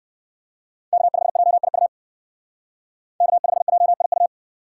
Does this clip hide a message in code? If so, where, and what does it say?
Yes: Morse code from 0.9 to 4.3 s, heard as a keyed tone, about 69 wpm, 700 Hz. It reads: K5JIU K5JIU